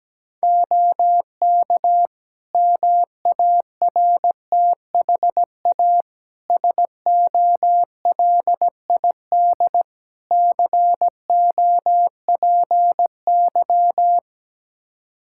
Morse code, heard as a keyed tone, about 17 wpm, 700 Hz. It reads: OK MARTHA SOLID COPY
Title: Practice QSO - Exchange 3 - More Info - 17wpm